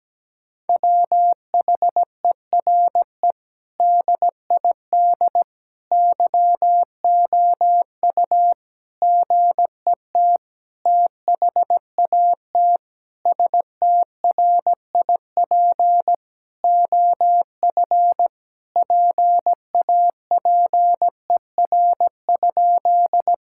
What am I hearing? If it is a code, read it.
Morse code, heard as a keyed tone, about 17 wpm, 700 Hz. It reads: WHERE DID YOU GET THAT STRIP OF PAPER?
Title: Sentences from Top 1600 Words 17wpm